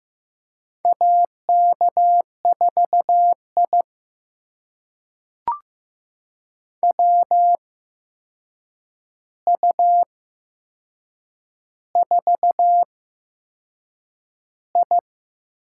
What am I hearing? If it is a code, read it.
Morse code, heard as a keyed tone, about 15 wpm, spaced as for 6 wpm, 700 Hz. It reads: AK4I WU4I
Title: USA Call Signs - 4 Characters - 8x Character Spacing - 15wpm